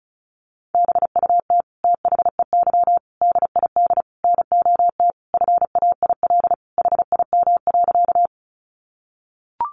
Morse code, heard as a keyed tone, about 34 wpm, 700 Hz. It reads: BUT THEY DID NOT FAIL HIM.
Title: Sentences from Top 1100 Words 35wpm